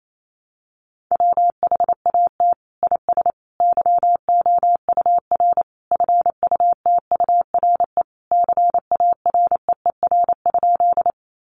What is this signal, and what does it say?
Morse code, heard as a keyed tone, about 28 wpm, 700 Hz. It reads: WHAT IS YOUR FUTURE CAREER?
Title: Sentences from Top 1000 Words 28wpm